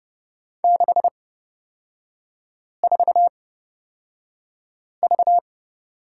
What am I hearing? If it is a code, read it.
Morse code, heard as a keyed tone, about 30 wpm, spaced as for 5 wpm, 690 Hz. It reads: 64V
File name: Lesson 38e - Letter q - Three Characters 30wpm